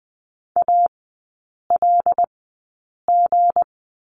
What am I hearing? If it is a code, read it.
Morse code, heard as a keyed tone, about 20 wpm, spaced as for 8 wpm, 700 Hz. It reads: ALG
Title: 178 Lesson 29e - Slash - Three Characters 20wpm